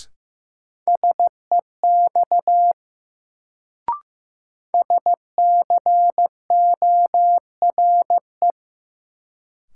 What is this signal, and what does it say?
Morse code, heard as a keyed tone, about 15 wpm, 700 Hz. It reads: SEX SCORE
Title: Sentences from Top 1600 Words - New Word Drill Alt - 15wpm